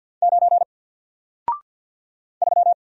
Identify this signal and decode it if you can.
Morse code, heard as a keyed tone, about 50 wpm, 700 Hz. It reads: 9 3
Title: Single Letter-Number Alt 50wpm